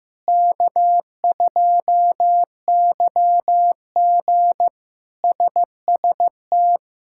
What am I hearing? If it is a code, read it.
Morse code, heard as a keyed tone, about 15 wpm, 700 Hz. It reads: K2YG SST
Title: K1USN SST QSO Practice - 15wpm